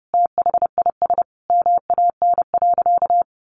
Morse code, heard as a keyed tone, about 30 wpm, 700 Hz. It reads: THIS MAN.
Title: Sentences from Top 1400 Words 30wpm